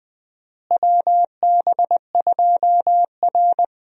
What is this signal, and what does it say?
Morse code, heard as a keyed tone, about 20 wpm, 700 Hz. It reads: WB2R
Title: USA Call Signs - 4 Characters - Repeated 3x - 20wpm